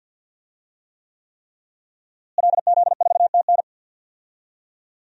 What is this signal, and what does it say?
Morse code, heard as a keyed tone, about 52 wpm, 700 Hz. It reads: LZ4TN
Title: DXCC-100 Call Signs - 5 Characters - Repeated 3x - 50wpm